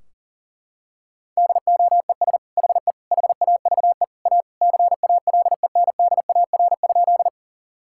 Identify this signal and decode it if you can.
Morse code, heard as a keyed tone, about 40 wpm, 700 Hz. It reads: DOES HE HAVE A CALENDAR?
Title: Sentences from Top 1600 Words 40wpm